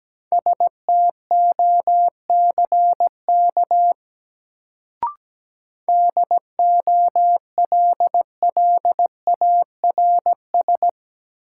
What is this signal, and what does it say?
Morse code, heard as a keyed tone, about 17 wpm, 700 Hz. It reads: STOCK DOLLARS
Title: Sets of 5 Words - New Word Drill - 17wpm